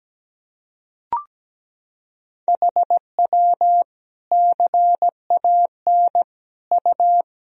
Morse code, heard as a keyed tone, about 17 wpm, 700 Hz. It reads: HW CAN U
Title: Sentences from Top 100 Words and Abbreviations 17wpm